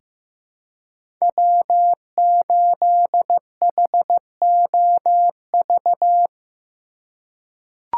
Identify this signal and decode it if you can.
Morse code, heard as a keyed tone, about 15 wpm, 700 Hz. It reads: W8HOV
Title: USA Call Signs - 5 Characters - 3x Character Spacing - 15wpm